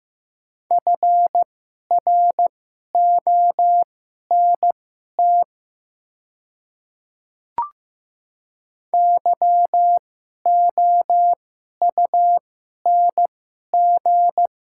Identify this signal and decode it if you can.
Morse code, heard as a keyed tone, about 15 wpm, spaced as for 11 wpm, 700 Hz. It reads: FRONT YOUNG
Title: 2x Character Spacing - Top 500 Words - 15wpm